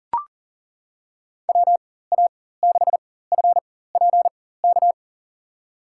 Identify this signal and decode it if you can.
Morse code, heard as a keyed tone, about 37 wpm, spaced as for 18 wpm, 700 Hz. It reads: WA6FPK
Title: USA Call Signs - 6 Characters - 4x Character Spacing - 40wpm